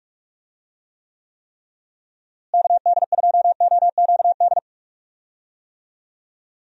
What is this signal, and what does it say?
Morse code, heard as a keyed tone, about 43 wpm, 700 Hz. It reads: KD2OQD
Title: USA Call Signs - 6 Characters - Repeated 3x - 45wpm